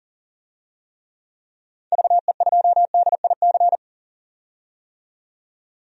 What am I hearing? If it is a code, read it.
Morse code, heard as a keyed tone, about 40 wpm, 700 Hz. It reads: VE2DIC
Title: DXCC-100 Call Signs - 6 Characters - 40wpm